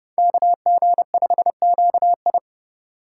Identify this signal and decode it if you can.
Morse code, heard as a keyed tone, about 31 wpm, 700 Hz. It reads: KG5QI